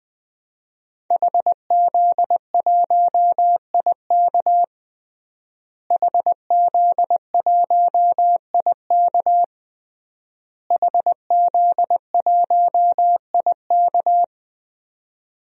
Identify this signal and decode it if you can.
Morse code, heard as a keyed tone, about 20 wpm, 700 Hz. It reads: HZ1IK HZ1IK HZ1IK
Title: DXCC-100 Call Signs - 5 Characters - Repeated 3x - 20wpm